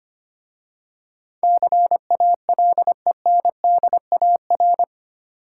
Morse code, heard as a keyed tone, about 25 wpm, 700 Hz. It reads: CALENDAR
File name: practice / 25wpm / Easy Words with Hints - 25wpm